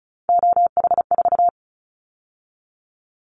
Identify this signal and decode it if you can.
Morse code, heard as a keyed tone, about 36 wpm, 700 Hz. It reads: OH4
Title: Callsign Prefixes for Top 25 DXCC - 35wpm